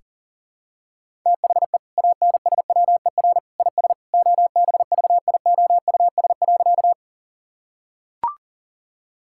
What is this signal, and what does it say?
Morse code, heard as a keyed tone, about 40 wpm, 700 Hz. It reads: THE ANSWER IS OBVIOUS.
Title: Sentences from Top 1600 Words 40wpm